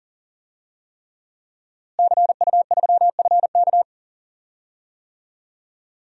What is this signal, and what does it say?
Morse code, heard as a keyed tone, about 40 wpm, 700 Hz. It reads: CU3FK
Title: DXCC-100 Call Signs - 5 Characters - Repeated 3x - 40wpm